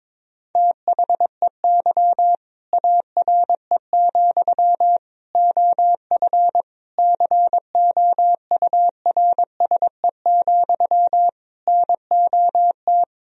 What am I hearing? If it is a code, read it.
Morse code, heard as a keyed tone, about 22 wpm, 700 Hz. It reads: THEY ARE, OF COURSE, NOT
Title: Sentences from Top 1000 Words - Review Part 4 - 22wpm